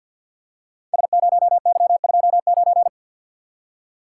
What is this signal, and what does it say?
Morse code, heard as a keyed tone, about 52 wpm, 700 Hz. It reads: S0Y29
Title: Code Groups - 5 Letter:Numbers - 50wpm